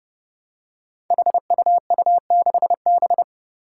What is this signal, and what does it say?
Morse code, heard as a keyed tone, about 29 wpm, 700 Hz. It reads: HUU6B
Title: Code Groups - 5 Letter:Numbers - 30wpm